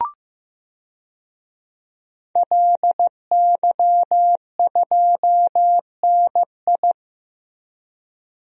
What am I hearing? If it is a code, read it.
Morse code, heard as a keyed tone, about 15 wpm, 700 Hz. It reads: LY2NI
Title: DXCC-100 Call Signs - 5 Characters - 15wpm